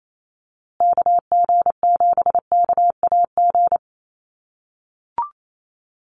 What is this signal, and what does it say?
Morse code, heard as a keyed tone, about 28 wpm, 690 Hz. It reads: KG7KAG
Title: USA Call Signs Repeated 3x 28wpm